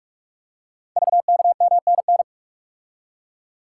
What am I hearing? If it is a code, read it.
Morse code, heard as a keyed tone, about 44 wpm, 690 Hz. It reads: VKMNN
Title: Code Groups - 5 Letters - 45wpm